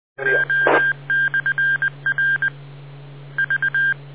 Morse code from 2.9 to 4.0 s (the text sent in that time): V